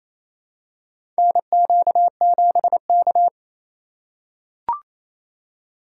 Morse code, heard as a keyed tone, about 28 wpm, 700 Hz. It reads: NQ7K
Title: USA Call Signs - 4 Characters - 2x Character Spacing - 28wpm